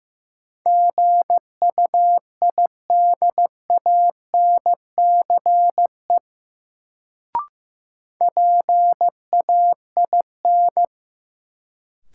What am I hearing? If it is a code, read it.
Morse code, heard as a keyed tone, about 15 wpm, 700 Hz. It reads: GUIDANCE PAIN
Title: Sentences from Top 1600 Words - New Word Drill Alt - 15wpm